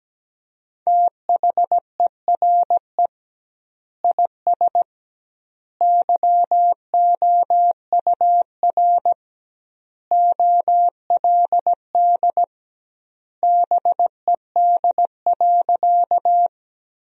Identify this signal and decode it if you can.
Morse code, heard as a keyed tone, about 17 wpm, 700 Hz. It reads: THERE IS YOUR OLD BED.
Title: 2x Word-Spacing - Sentences from Top 500 Words - 17wpm